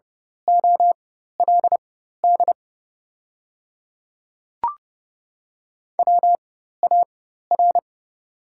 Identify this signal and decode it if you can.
Morse code, heard as a keyed tone, about 30 wpm, spaced as for 14 wpm, 690 Hz. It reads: OLD WAR
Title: 4x Character Spacing - Top 500 Words - 30wpm